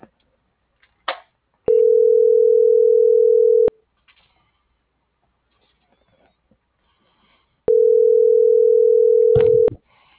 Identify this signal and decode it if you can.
Ringback tone (North America), 440 + 480 Hz.